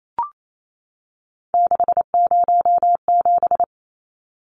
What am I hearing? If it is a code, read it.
Morse code, heard as a keyed tone, about 28 wpm, 700 Hz. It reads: 607
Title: Numbers - 3-Digit 28wpm